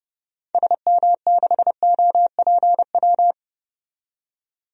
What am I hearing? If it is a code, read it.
Morse code, heard as a keyed tone, about 30 wpm, 700 Hz. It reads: SM6OPW
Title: DXCC-100 Call Signs - 6 Characters - 30wpm